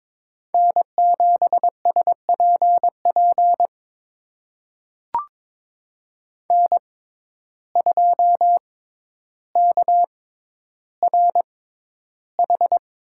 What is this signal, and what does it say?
Morse code, heard as a keyed tone, about 22 wpm, spaced as for 11 wpm, 700 Hz. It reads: N7SPP N2KRH
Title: USA Call Signs - 6x Character Spacing - 22wpm